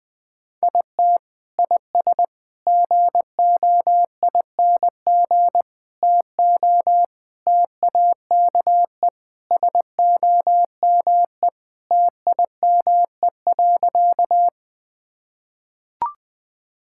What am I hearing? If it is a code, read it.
Morse code, heard as a keyed tone, about 21 wpm, 700 Hz. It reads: IT IS GOING TO TAKE SOME TIME.
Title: Sentences with 3-Word Phrases - Spoken Beginning - 20wpm